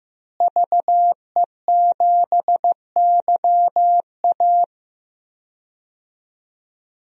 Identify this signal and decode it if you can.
Morse code, heard as a keyed tone, about 15 wpm, 700 Hz. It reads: VE7YA
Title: DXCC-100 Call Signs - 5 Characters - Repeated 2x - 15wpm